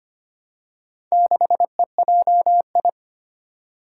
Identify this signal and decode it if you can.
Morse code, heard as a keyed tone, about 25 wpm, 700 Hz. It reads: 6EJI
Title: Four Challenging Letter-Number Combinations 25wpm